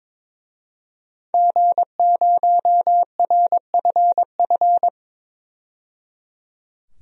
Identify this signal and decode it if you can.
Morse code, heard as a keyed tone, about 22 wpm, 700 Hz. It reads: G0RFF
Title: DXCC-100 Call Signs - 5 Characters - 22wpm